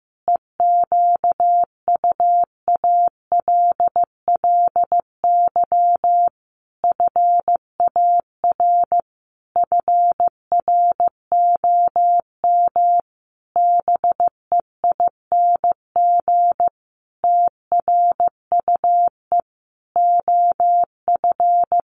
Morse code, heard as a keyed tone, about 15 wpm, 700 Hz. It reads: EQUALLY FAR FROM BEING TRUE OF
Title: Sentences from Top 1400 Words 15wpm